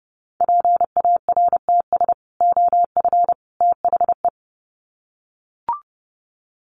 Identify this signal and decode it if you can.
Morse code, heard as a keyed tone, about 30 wpm, 700 Hz. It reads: PARTS OF THE
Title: Sets of 3 Words 30wpm